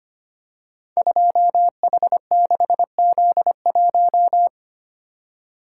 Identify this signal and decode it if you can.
Morse code, heard as a keyed tone, about 25 wpm, 700 Hz. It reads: 2H6Z1
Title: Code Groups - 5 Letter:Numbers - 25wpm